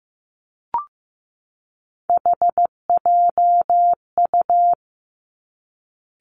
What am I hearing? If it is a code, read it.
Morse code, heard as a keyed tone, about 15 wpm, 700 Hz. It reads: HJU
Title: Three Challenging Letter-Number Combinations 15wpm